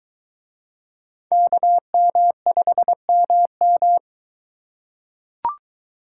Morse code, heard as a keyed tone, about 23 wpm, 700 Hz. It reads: KM5MM